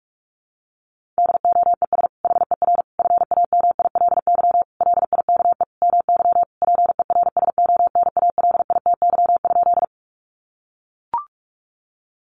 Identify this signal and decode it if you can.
Morse code, heard as a keyed tone, about 44 wpm, 700 Hz. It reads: DOES HER FAMILY LIKE MY PERSONALITY?
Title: Sentences from Top 1400 Words 45wpm